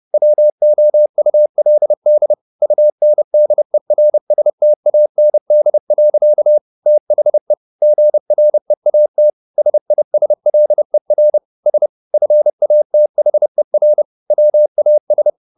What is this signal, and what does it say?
Morse code, heard as a keyed tone, about 30 wpm, 600 Hz. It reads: WOULD UNDERSTAND. THE GREAT SISLER S FATHER WAS
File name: the old man and the sea / the old man and the sea 1 30wpm